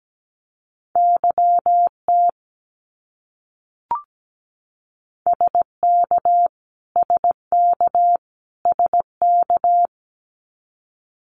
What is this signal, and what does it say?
Morse code, heard as a keyed tone, about 17 wpm, 700 Hz. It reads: YT SK SK SK
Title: Canadian Provinces and Territories - Repeated 3x - 17wpm